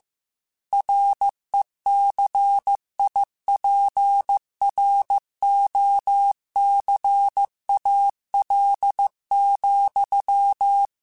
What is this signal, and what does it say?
Morse code, heard as a keyed tone, about 15 wpm, 800 Hz. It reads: RECIPROCAL,